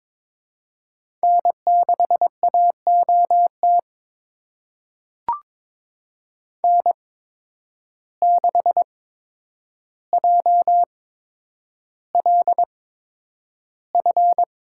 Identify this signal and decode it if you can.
Morse code, heard as a keyed tone, about 22 wpm, spaced as for 9 wpm, 700 Hz. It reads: N6AOT N6JLF
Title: USA Call Signs - 5 Characters - 8x Character Spacing - 22wpm